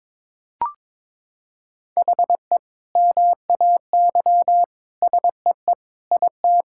Morse code, heard as a keyed tone, about 22 wpm, 700 Hz. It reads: HE MAY SEE IT
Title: Sentences from Top 1600 Words 22wpm